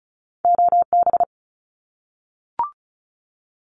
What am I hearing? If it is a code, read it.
Morse code, heard as a keyed tone, about 34 wpm, 700 Hz. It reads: OB
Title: Abbreviations Repeated 3x - 35wpm